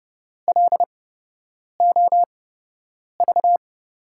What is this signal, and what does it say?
Morse code, heard as a keyed tone, about 30 wpm, spaced as for 8 wpm, 700 Hz. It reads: LOV
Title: USA Call Signs - 6 Characters - 8x Character Spacing - 30wpm